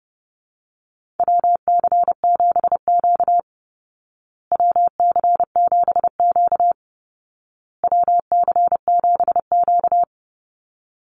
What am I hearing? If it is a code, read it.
Morse code, heard as a keyed tone, about 30 wpm, 700 Hz. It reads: WC7Q WC7Q WC7Q
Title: Call Signs Repeated 2x - 30wpm